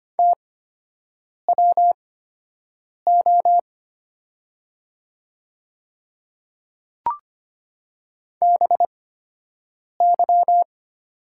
Morse code, heard as a keyed tone, about 25 wpm, spaced as for 7 wpm, 700 Hz. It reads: TWO BY
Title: 8x Character Spacing - Top 100 Words - 25wpm